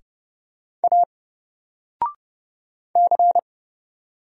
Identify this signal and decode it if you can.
Morse code, heard as a keyed tone, about 31 wpm, 700 Hz. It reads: A C
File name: Single-Letter-Number-30wpm